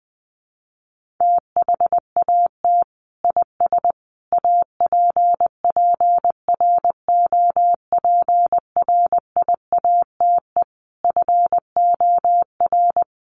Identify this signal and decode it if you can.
Morse code, heard as a keyed tone, about 20 wpm, 700 Hz. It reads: THAT IS APPROPRIATE FOR
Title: Sentences from Top 900 Words 20wpm